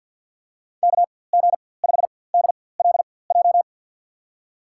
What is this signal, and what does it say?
Morse code, heard as a keyed tone, about 50 wpm, spaced as for 23 wpm, 700 Hz. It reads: KG5DLJ
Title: USA Call Signs - 6 Characters - 4x Character Spacing - 50wpm